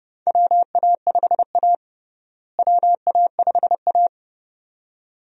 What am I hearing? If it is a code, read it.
Morse code, heard as a keyed tone, about 32 wpm, 700 Hz. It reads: WA5A WA5A
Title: USA Call Signs - 4 Characters - Repeated 2x - 30wpm